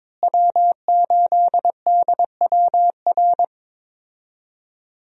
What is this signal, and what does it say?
Morse code, heard as a keyed tone, about 22 wpm, 700 Hz. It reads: W8DWR
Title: USA Call Signs - 5 Characters - Repeated 3x - 22wpm